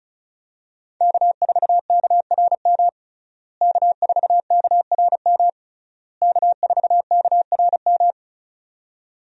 Morse code, heard as a keyed tone, about 36 wpm, 700 Hz. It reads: K4KRM K4KRM K4KRM
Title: USA Call Signs - 5 Characters - Repeated 3x - 35wpm